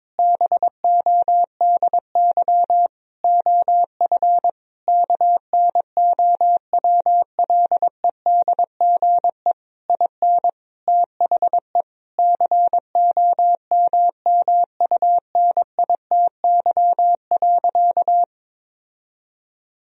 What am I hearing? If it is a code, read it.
Morse code, heard as a keyed tone, about 21 wpm, 700 Hz. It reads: BODY OF KNOWLEDGE IN THE COMMUNITY.
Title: Sentences from Top 1000 Words - Review Part 4 - 22wpm